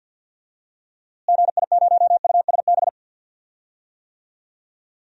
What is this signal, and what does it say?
Morse code, heard as a keyed tone, about 48 wpm, 700 Hz. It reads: GI0USB
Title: DXCC-100 Call Signs - 6 Characters - Repeated 3x - 50wpm